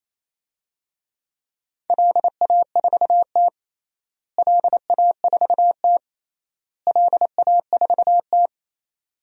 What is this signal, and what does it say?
Morse code, heard as a keyed tone, about 28 wpm, 700 Hz. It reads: LA4T LA4T LA4T
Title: DXCC-100 Call Signs - 4 Characters - Repeated 3x - 28wpm